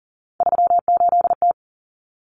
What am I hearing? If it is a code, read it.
Morse code, heard as a keyed tone, about 40 wpm, 700 Hz. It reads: 38T
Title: Common Cut Numbers - 3-Digit 40wpm